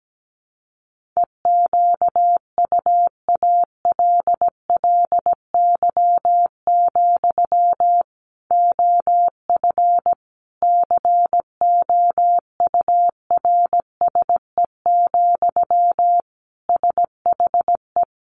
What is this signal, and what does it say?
Morse code, heard as a keyed tone, about 17 wpm, 700 Hz. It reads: EQUALLY, OF COURSE, SHE
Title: Sentences from Top 1400 Words 17wpm